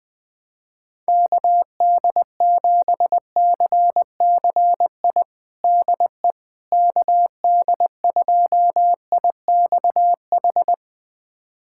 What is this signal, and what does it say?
Morse code, heard as a keyed tone, about 21 wpm, 700 Hz. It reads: KD7CCI DE KD2IXH
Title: USA Callsign DE Callsign - 6 Characters - 20wpm